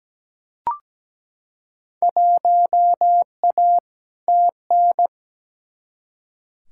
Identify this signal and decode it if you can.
Morse code, heard as a keyed tone, about 17 wpm, 700 Hz. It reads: 1A TN
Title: Field Day Exchanges - 17wpm